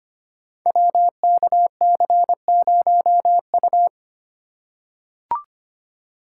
Morse code, heard as a keyed tone, about 25 wpm, 700 Hz. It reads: WKC0U
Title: Code Groups - 5 Letter:Numbers - 25wpm